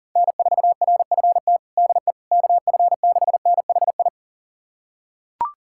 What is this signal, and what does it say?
Morse code, heard as a keyed tone, about 40 wpm, 700 Hz. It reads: N4RFT DE KF6NHI
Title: USA Callsign DE Callsign 40wpm